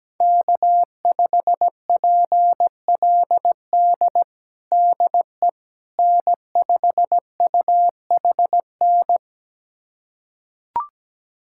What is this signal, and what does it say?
Morse code, heard as a keyed tone, about 17 wpm, 700 Hz. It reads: K5PLD DE N5UHN
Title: USA Callsign DE Callsign - 5 Characters - 17wpm